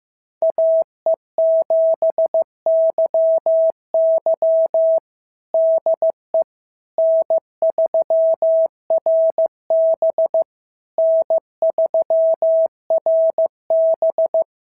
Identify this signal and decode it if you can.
Morse code, heard as a keyed tone, about 15 wpm, 650 Hz. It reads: AE7YY DE N3RB N3RB
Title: SOTA QSO Practice 15wpm